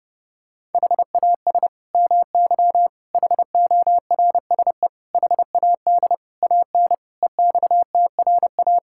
Morse code, heard as a keyed tone, about 30 wpm, 700 Hz. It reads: HAS MY HORSE HAD AN EXTRA
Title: Sentences from Top 1600 Words 30wpm